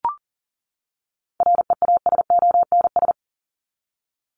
Morse code, heard as a keyed tone, about 40 wpm, 700 Hz. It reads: REASONS
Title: ES:S Suffix - Top 100 Words with Base Words 40wpm